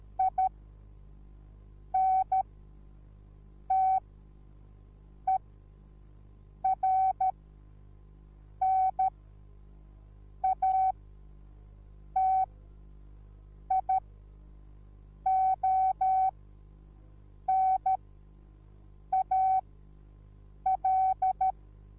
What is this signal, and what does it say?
Morse code, heard as a keyed tone, about 13 wpm, spaced as for 5 wpm, 760 Hz. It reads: INTERNATIONAL